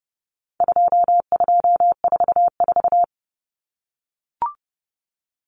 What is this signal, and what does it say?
Morse code, heard as a keyed tone, about 30 wpm, 700 Hz. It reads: 2244
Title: Common Cut Numbers - 4-Digit 30wpm